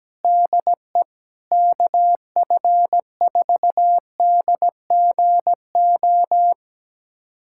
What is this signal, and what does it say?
Morse code, heard as a keyed tone, about 17 wpm, 700 Hz. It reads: DE KF4DGO
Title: USA Callsign DE Callsign - 6 Characters - 17wpm